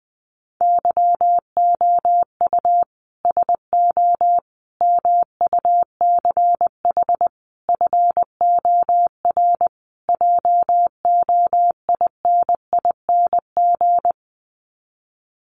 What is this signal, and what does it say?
Morse code, heard as a keyed tone, about 20 wpm, 700 Hz. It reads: YOU SO MUCH FOR JOINING
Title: Sets of 5 Words Encore2 20wpm